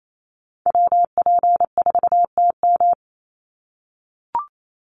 Morse code, heard as a keyed tone, about 28 wpm, 700 Hz. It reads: WP4TM
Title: USA Call Signs 28wpm